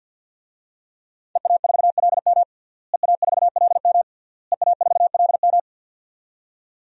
Morse code, heard as a keyed tone, about 47 wpm, 700 Hz. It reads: EA4LM EA4LM EA4LM
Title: DXCC-100 Call Signs - 5 Characters - Repeated 3x - 50wpm